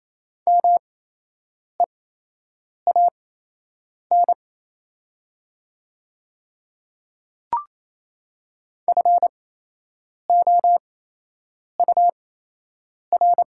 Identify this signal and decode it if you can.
Morse code, heard as a keyed tone, about 27 wpm, spaced as for 8 wpm, 690 Hz. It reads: MEAN FOUR
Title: 8x Character Spacing - Top 300 Words - 28wpm